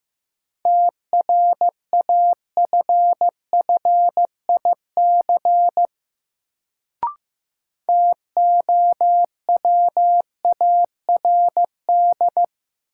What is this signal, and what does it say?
Morse code, heard as a keyed tone, about 16 wpm, 700 Hz. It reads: TRAFFIC TOWARD
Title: Sentences from Top 1300 Words - New Word Drill Alt - 15wpm